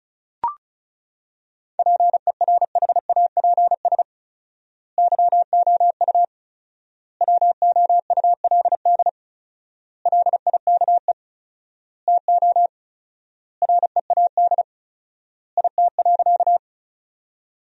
Morse code, heard as a keyed tone, about 34 wpm, 700 Hz. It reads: PERHAPS YOU WOULD LIKE TO READ IT.
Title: 4x Word-Spacing - Sentences from Top 500 Words - 35wpm